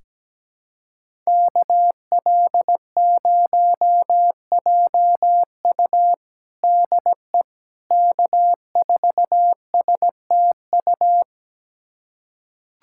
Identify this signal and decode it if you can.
Morse code, heard as a keyed tone, about 17 wpm, 700 Hz. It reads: KL0JU DE K4STU
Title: USA Callsign DE Callsign - 5 Characters - 17wpm